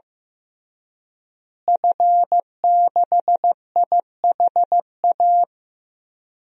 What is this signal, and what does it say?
Morse code, heard as a keyed tone, about 15 wpm, 700 Hz. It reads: F6IHA
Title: DXCC-100 Call Signs - 5 Characters - Repeated 3x - 15wpm